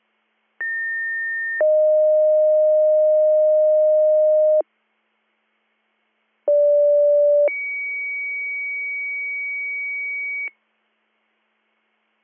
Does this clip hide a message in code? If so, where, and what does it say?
Yes: two-tone page from 0.6 to 10.5 s, heard as a one-second tone, then a three-second tone, about 1800 then 620 Hz, about 580 then 2250 Hz.